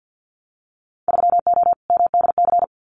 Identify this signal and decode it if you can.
Morse code, heard as a keyed tone, about 50 wpm, 700 Hz. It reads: 3O MDC
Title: Winter Field Day Exchanges Encore 50wpm